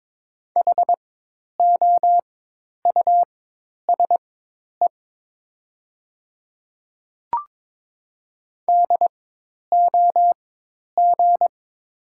Morse code, heard as a keyed tone, about 22 wpm, spaced as for 10 wpm, 700 Hz. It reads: HOUSE DOG
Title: 4x Character Spacing - Top 500 Words - 22wpm